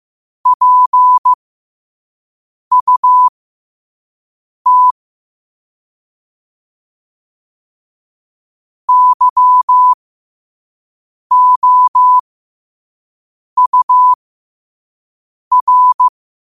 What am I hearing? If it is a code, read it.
Morse code, heard as a keyed tone, about 15 wpm, spaced as for 5 wpm, 1000 Hz. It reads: PUT YOUR
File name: qotdfiles / 5wpm